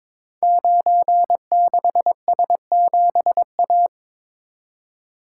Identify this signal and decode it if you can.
Morse code, heard as a keyed tone, about 22 wpm, 700 Hz. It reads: 96S7A
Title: Code Groups - 5 Letter:Numbers - 22wpm